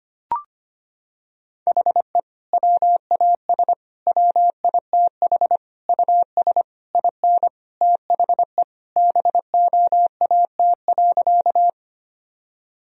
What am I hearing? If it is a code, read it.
Morse code, heard as a keyed tone, about 25 wpm, 700 Hz. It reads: HE WAS WITH US IN THE BOAT.
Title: Sentences from Top 1000 Words - Review Part 3 - 25wpm